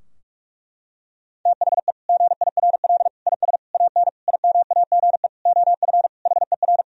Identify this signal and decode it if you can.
Morse code, heard as a keyed tone, about 41 wpm, 700 Hz. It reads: THE GIRL IS AN IMAGE OF HER
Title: Sentences from Top 1000 Words - Review Part 4 - 45wpm